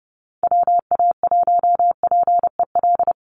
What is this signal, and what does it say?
Morse code, heard as a keyed tone, about 30 wpm, 700 Hz. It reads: WA1PEL
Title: USA Call Signs 30wpm